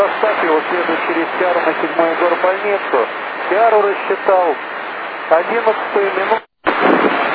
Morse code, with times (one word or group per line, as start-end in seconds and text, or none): none